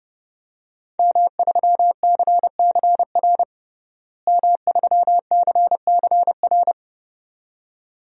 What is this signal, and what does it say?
Morse code, heard as a keyed tone, about 30 wpm, 700 Hz. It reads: M3CCR M3CCR
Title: DXCC-100 Call Signs - 5 Characters - Repeated 2x - 30wpm